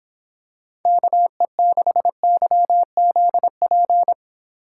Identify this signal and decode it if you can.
Morse code, heard as a keyed tone, about 26 wpm, 700 Hz. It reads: KE6YZP